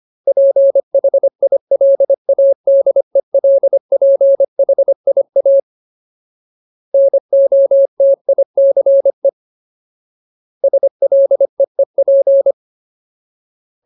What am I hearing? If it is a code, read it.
Morse code, heard as a keyed tone, about 25 wpm, 550 Hz. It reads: PHILADELPHIA NOTICE SLEEP
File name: Ultimate Word List / 25WPM Ultimate